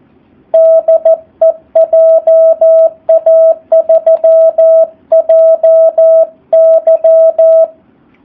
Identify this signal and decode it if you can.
Morse code, heard as a keyed tone, about 14 wpm, 650 Hz. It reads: DE JA3JY
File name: jh8cbh / 2019cw / SP04- w